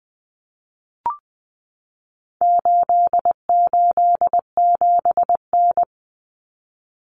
Morse code, heard as a keyed tone, about 20 wpm, 700 Hz. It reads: 887N
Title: Common Cut Numbers - 4-Digit 20wpm